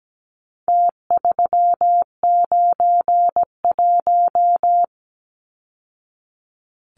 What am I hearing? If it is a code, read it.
Morse code, heard as a keyed tone, about 17 wpm, 700 Hz. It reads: T391
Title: Numbers - Contest Style - 4-Digit 17wpm